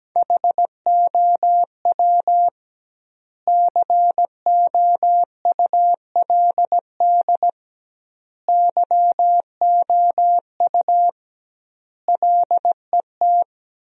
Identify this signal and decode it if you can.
Morse code, heard as a keyed tone, about 17 wpm, 700 Hz. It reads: HOW COULD YOU LET
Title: 2x Word-Spacing - Sentences from Top 300 Words - 17wpm